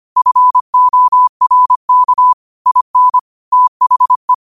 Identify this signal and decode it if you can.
Morse code, heard as a keyed tone, about 25 wpm, 1000 Hz. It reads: FORK IN THE
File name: qotdfiles / 25wpm